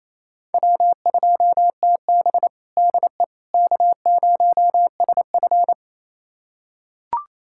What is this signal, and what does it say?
Morse code, heard as a keyed tone, about 28 wpm, 700 Hz. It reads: W2TB DE K0SF
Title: USA Callsign DE Callsign - 4 Characters - 28wpm